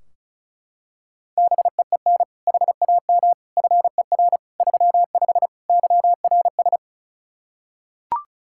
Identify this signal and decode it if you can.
Morse code, heard as a keyed tone, about 34 wpm, 700 Hz. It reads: BEEN HAM FER 35 YRS
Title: Practice QSO - Exchange 3 - More Info - 35wpm